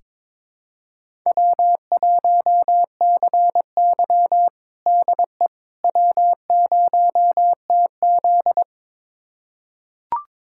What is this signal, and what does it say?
Morse code, heard as a keyed tone, about 22 wpm, 700 Hz. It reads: W1CY DE W0TZ
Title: USA Callsign DE Callsign - 4 Characters - 22wpm